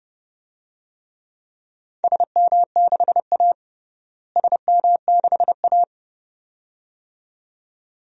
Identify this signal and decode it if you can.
Morse code, heard as a keyed tone, about 30 wpm, 700 Hz. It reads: SM6A SM6A
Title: DXCC-100 Call Signs - 4 Characters - Repeated 2x - 30wpm